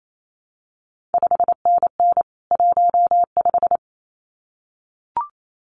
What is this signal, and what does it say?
Morse code, heard as a keyed tone, about 28 wpm, 700 Hz. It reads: 5NN 15
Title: CQ WW DX Contest Exchange Encore - 28wpm